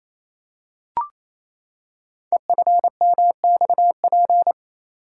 Morse code, heard as a keyed tone, about 27 wpm, 700 Hz. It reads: EFMXP
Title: Code Groups - 5 Letters - 28wpm